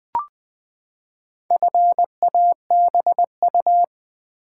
Morse code, heard as a keyed tone, about 20 wpm, 700 Hz. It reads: FABU
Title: Build A Word - 4 Letters At A Time - 8-Letter Words - 20wpm